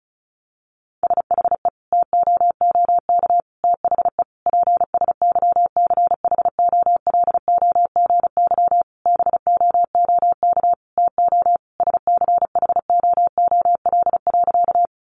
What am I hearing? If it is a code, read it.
Morse code, heard as a keyed tone, about 34 wpm, 700 Hz. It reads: SHE TOOK THE PSYCHOLOGY BOOK TO SCHOOL.
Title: Sentences from Top 1400 Words 35wpm